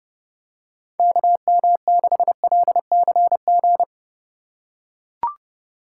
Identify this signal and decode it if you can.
Morse code, heard as a keyed tone, about 31 wpm, 700 Hz. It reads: KM6LCG